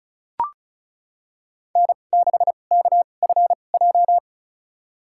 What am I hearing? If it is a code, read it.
Morse code, heard as a keyed tone, about 34 wpm, spaced as for 25 wpm, 700 Hz. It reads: N6KFJ
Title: USA Call Signs - 5 Characters - 2x Character Spacing - 35wpm